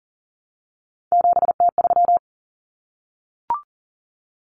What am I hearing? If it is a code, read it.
Morse code, heard as a keyed tone, about 40 wpm, 700 Hz. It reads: 7T3
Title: Common Cut Numbers - 3-Digit 40wpm